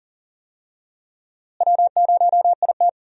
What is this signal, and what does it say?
Morse code, heard as a keyed tone, about 40 wpm, 700 Hz. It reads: W0IT
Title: USA Call Signs - 4 Characters - Repeated 2x - 40wpm